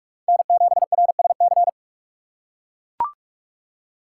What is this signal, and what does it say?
Morse code, heard as a keyed tone, about 40 wpm, 700 Hz. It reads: N7RSC